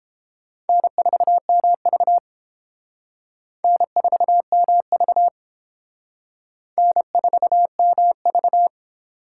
Morse code, heard as a keyed tone, about 28 wpm, 700 Hz. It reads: N4MV N4MV N4MV